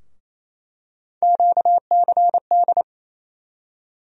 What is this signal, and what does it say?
Morse code, heard as a keyed tone, about 28 wpm, 700 Hz. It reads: QCD
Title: USA Call Sign Postfixes - 3 Letters 28wpm